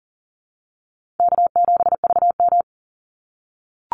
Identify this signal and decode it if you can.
Morse code, heard as a keyed tone, about 37 wpm, 700 Hz. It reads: K7VM
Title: Easy Call Signs - 40wpm